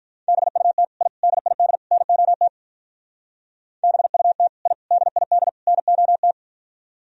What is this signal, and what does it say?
Morse code, heard as a keyed tone, about 46 wpm, 700 Hz. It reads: BUT I DID NOT BUT I DID NOT